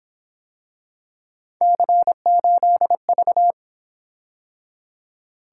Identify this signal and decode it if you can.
Morse code, heard as a keyed tone, about 27 wpm, 700 Hz. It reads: C8V
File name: Speed-Racing - Three Letter-Number Combinations - 22wpm